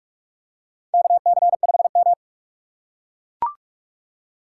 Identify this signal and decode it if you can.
Morse code, heard as a keyed tone, about 44 wpm, 690 Hz. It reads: KC5M
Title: USA Call Signs - 4 Characters - 3x Character Spacing - 45wpm